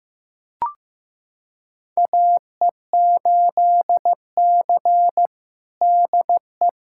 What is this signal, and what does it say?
Morse code, heard as a keyed tone, about 16 wpm, 700 Hz. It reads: AE8C DE
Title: USA Callsign DE Callsign - 4 Characters - 15wpm